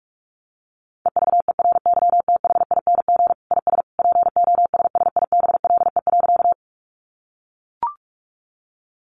Morse code, heard as a keyed tone, about 44 wpm, 700 Hz. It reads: EVERYTHING IS POSSIBLE.
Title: Sentences from Top 700 Words 45wpm